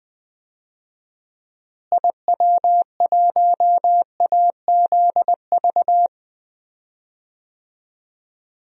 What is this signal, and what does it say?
Morse code, heard as a keyed tone, about 20 wpm, 700 Hz. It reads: IW1AZV
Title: DXCC-100 Call Signs - 6 Characters - Repeated 3x - 20wpm